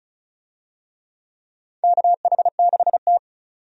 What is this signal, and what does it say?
Morse code, heard as a keyed tone, about 34 wpm, 700 Hz. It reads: KH6T
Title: USA Call Signs - 4 Characters - 35wpm